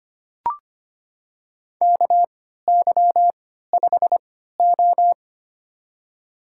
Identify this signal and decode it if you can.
Morse code, heard as a keyed tone, about 25 wpm, spaced as for 14 wpm, 700 Hz. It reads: KY5O
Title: USA Call Signs - 3x Character Spacing - 25wpm